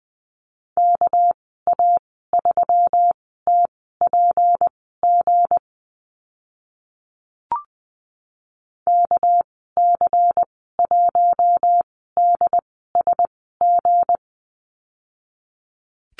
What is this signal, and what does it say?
Morse code, heard as a keyed tone, about 20 wpm, spaced as for 14 wpm, 700 Hz. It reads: KA3TPG KC1DSG
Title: 213 Lesson 34g - Number 8 - Call Signs 20wpm